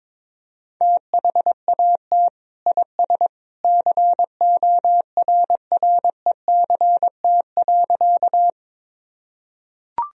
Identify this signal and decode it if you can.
Morse code, heard as a keyed tone, about 21 wpm, 700 Hz. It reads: THAT IS CORRECT.
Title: Sentences from Top 1000 Words - Review Part 2 - 22wpm